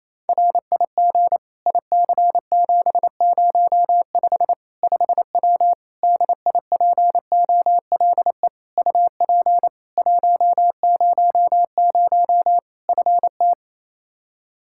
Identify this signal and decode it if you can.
Morse code, heard as a keyed tone, about 28 wpm, 700 Hz. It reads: RIG IC705 5W DIPOLE UP 100 FT
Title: K1USN SST QSO Practice 28wpm